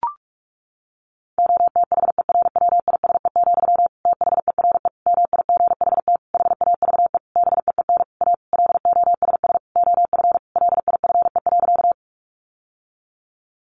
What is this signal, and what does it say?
Morse code, heard as a keyed tone, about 44 wpm, 700 Hz. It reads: OTHERWISE, THERE MIGHT HAVE BEEN A LOSS OF LIFE.
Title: Sentences from Top 1400 Words 45wpm